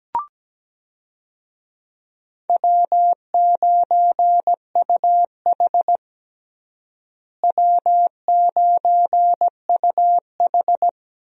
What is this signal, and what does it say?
Morse code, heard as a keyed tone, about 17 wpm, 700 Hz. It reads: W9UH W9UH
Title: USA Call Signs - 4 Characters - Repeated 2x - 17wpm